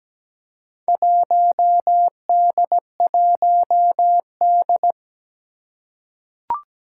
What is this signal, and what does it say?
Morse code, heard as a keyed tone, about 17 wpm, 700 Hz. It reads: 1D1D
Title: Four Challenging Letter-Number Combinations 17wpm